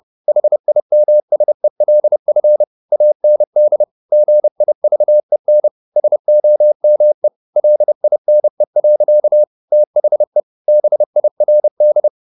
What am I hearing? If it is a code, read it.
Morse code, heard as a keyed tone, about 30 wpm, 600 Hz. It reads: HIMSELF AND GIVEN SOME LINE. THE BIRD
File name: the old man and the sea / the old man and the sea 1 30wpm